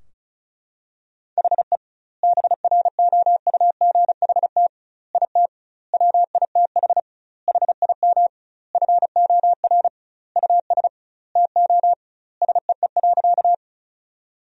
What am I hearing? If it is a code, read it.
Morse code, heard as a keyed tone, about 34 wpm, 700 Hz. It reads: HE BROUGHT IT WITH HIM FOR US TO SEE.
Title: 2x Word-Spacing - Sentences from Top 500 Words - 35wpm